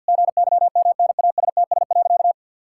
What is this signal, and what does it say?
Morse code, heard as a keyed tone, about 50 wpm, 700 Hz. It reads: GYMNASTS.
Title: Pangram Challenge v2 - 50wpm